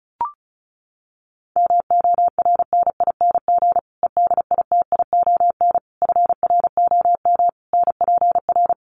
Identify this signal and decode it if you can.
Morse code, heard as a keyed tone, about 34 wpm, 700 Hz. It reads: MORNING EDITION FROM NPR
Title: Sets of 5 Words Encore 35wpm